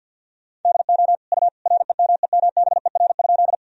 Morse code, heard as a keyed tone, about 49 wpm, 700 Hz. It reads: DO U REMEMBER?
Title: Sentences from Top 400 Words and Abbreviations 50wpm